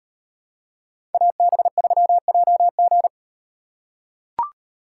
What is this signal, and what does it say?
Morse code, heard as a keyed tone, about 38 wpm, 700 Hz. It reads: AB3JG